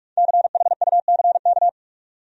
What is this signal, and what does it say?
Morse code, heard as a keyed tone, about 41 wpm, 700 Hz. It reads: CHUCK
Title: USA Common Names - 45wpm